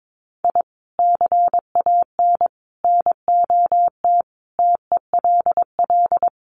Morse code, heard as a keyed tone, about 22 wpm, 700 Hz. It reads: I CAN NOT TELL
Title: Sets of 5 Words 22wpm